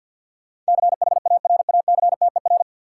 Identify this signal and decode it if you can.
Morse code, heard as a keyed tone, about 51 wpm, 700 Hz. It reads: CHARACTER
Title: Sentences from Top 1200 Words - New Word Drill - 50wpm